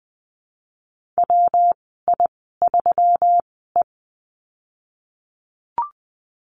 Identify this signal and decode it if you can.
Morse code, heard as a keyed tone, about 20 wpm, spaced as for 15 wpm, 700 Hz. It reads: WI3E
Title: 193 Lesson 31g - Letter v - Call Signs 20wpm